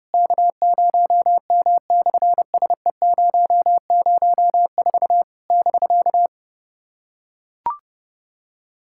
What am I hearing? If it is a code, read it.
Morse code, heard as a keyed tone, about 30 wpm, 700 Hz. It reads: K0M/SE004 <BK>
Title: SOTA QSO Practice 30wpm